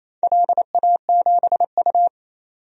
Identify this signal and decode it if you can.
Morse code, heard as a keyed tone, about 28 wpm, 700 Hz. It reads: LA7U
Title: DXCC-100 Call Signs - 4 Characters - 28wpm